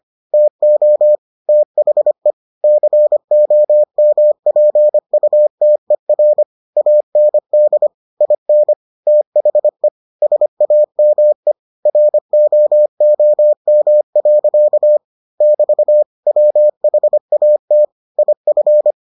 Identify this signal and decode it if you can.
Morse code, heard as a keyed tone, about 25 wpm, 600 Hz. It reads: TO THE COMPUTER AND IN THE SAME ROOM. <BT> WHAT IF